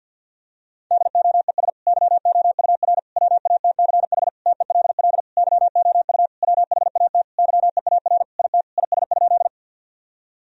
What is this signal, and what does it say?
Morse code, heard as a keyed tone, about 50 wpm, 700 Hz. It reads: DOES YOUR WATCH TELL YOU WHAT YEAR IT IS?
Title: Sentences from Top 1000 Words - Review Part 1 - 50wpm